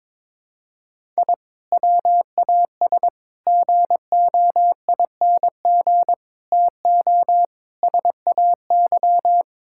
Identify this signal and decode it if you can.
Morse code, heard as a keyed tone, about 22 wpm, 700 Hz. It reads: I WAS GOING TO SAY
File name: practice / 22wpm / Build A Phrase - 5-Word Phrases - 22wpm